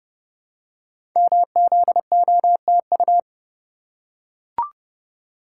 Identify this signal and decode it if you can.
Morse code, heard as a keyed tone, about 30 wpm, 700 Hz. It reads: MZOTU
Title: Code Groups - 5 Letters - 30wpm